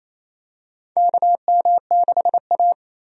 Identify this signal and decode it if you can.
Morse code, heard as a keyed tone, about 28 wpm, 700 Hz. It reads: KM6A
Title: USA Call Signs - 4 Characters - 28wpm